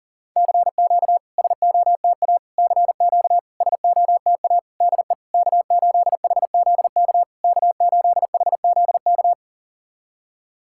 Morse code, heard as a keyed tone, about 40 wpm, 700 Hz. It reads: CQ SOTA CQ SOTA DE K8HZK K8HZK
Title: SOTA QSO Practice 40wpm